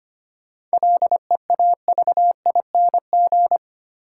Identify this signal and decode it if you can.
Morse code, heard as a keyed tone, about 25 wpm, 700 Hz. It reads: LEAVING
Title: Sentences from Top 1000 Words - Review Part 4 - New Word Drill - 25wpm